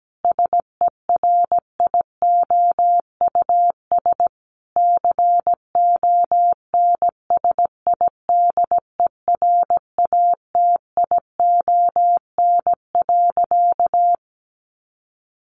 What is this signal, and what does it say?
Morse code, heard as a keyed tone, about 17 wpm, 700 Hz. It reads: SERIOUS CONSIDERATION.
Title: Sentences from Top 1500 Words 17wpm